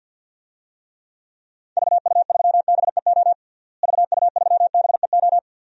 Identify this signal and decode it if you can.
Morse code, heard as a keyed tone, about 46 wpm, 700 Hz. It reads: VU3BEO VU3BEO
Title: DXCC-100 Call Signs - 6 Characters - Repeated 2x - 50wpm